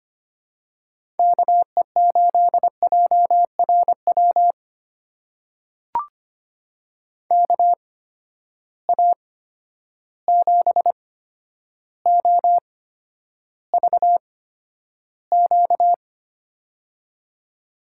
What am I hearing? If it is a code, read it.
Morse code, heard as a keyed tone, about 25 wpm, spaced as for 11 wpm, 700 Hz. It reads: KE8JRW KA7OVQ
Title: USA Call Signs - 6 Characters - 8x Character Spacing - 25wpm